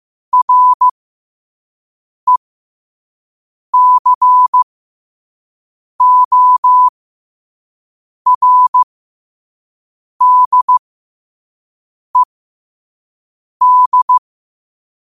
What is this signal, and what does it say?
Morse code, heard as a keyed tone, about 15 wpm, spaced as for 5 wpm, 1000 Hz. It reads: RECORDED